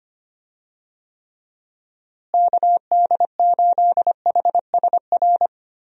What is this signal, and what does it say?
Morse code, heard as a keyed tone, about 25 wpm, 700 Hz. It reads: KD8HSR
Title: USA Call Signs - 6 Characters - Repeated 2x - 25wpm